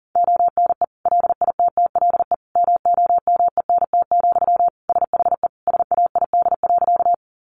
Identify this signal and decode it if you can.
Morse code, heard as a keyed tone, about 40 wpm, 700 Hz. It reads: ONE LITTLE MOMENT, SHE SAID.
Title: Sentences from Top 1000 Words 40wpm